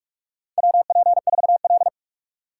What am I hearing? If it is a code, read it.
Morse code, heard as a keyed tone, about 45 wpm, 700 Hz. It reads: WP4L
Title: USA Call Signs - 3x Character Spacing - 45wpm